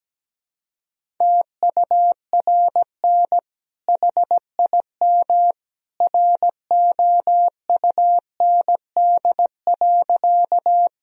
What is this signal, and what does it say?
Morse code, heard as a keyed tone, about 17 wpm, 700 Hz. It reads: TURN HIM ROUND.
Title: Sentences from Top 1000 Words - Review Part 1 - 17wpm